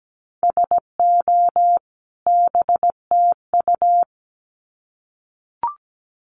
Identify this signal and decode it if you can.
Morse code, heard as a keyed tone, about 17 wpm, 700 Hz. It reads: SO BTU
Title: Common QSO Phrases 17wpm